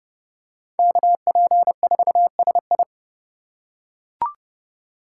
Morse code, heard as a keyed tone, about 32 wpm, 700 Hz. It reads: KP4SI
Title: USA Call Signs - 5 Characters - 4x Character Spacing - 30wpm